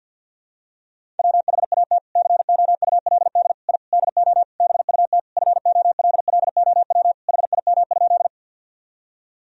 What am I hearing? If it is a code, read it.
Morse code, heard as a keyed tone, about 52 wpm, 700 Hz. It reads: WHAT COULD I DO BUT FOLLOW HIM?
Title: Sentences from Top 1000 Words - Review Part 1 - 50wpm